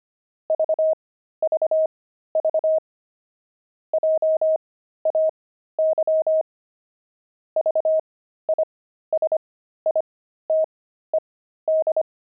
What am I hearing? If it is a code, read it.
Morse code, heard as a keyed tone, about 25 wpm, spaced as for 13 wpm, 630 Hz. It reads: VVV JAY VISITED
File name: pangram-fs-13-cs-25-jay